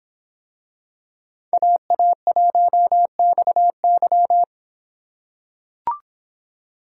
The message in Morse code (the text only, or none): AA1XY